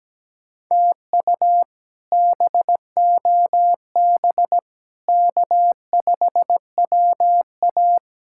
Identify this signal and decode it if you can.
Morse code, heard as a keyed tone, about 17 wpm, 700 Hz. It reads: TU BOB K5WA
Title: K1USN SST QSO Practice - 17wpm